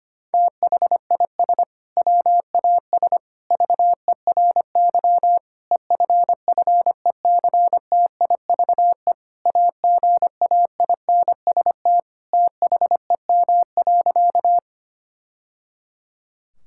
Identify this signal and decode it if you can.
Morse code, heard as a keyed tone, about 25 wpm, 700 Hz. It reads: THIS WAS VERY EFFECTIVE AGAINST THEM.
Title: Sentences from Top 1000 Words - Review Part 6 - 25wpm